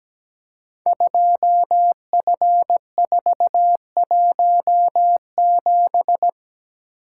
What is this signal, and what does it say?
Morse code, heard as a keyed tone, about 17 wpm, 700 Hz. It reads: 2F417
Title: Code Groups - 5 Letter:Numbers - 17wpm